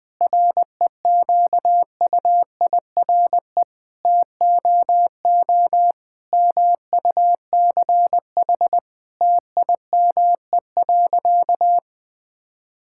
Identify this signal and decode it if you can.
Morse code, heard as a keyed tone, about 20 wpm, 700 Hz. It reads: REQUIRE TOO MUCH TIME.
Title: Sentences from Top 1000 Words - Review Part 6 - 20wpm